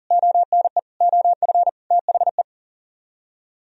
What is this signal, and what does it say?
Morse code, heard as a keyed tone, about 56 wpm, spaced as for 48 wpm, 700 Hz. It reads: ONE OF THE
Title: Speed-Racing - Sets of 4 Words 40wpm